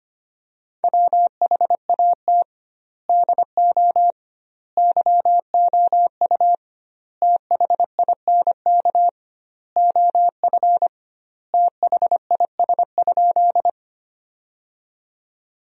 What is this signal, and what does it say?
Morse code, heard as a keyed tone, about 25 wpm, 700 Hz. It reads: WHAT DO YOU THINK OF THIS?
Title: 2x Word-Spacing - Sentences from Top 200 Words - 25wpm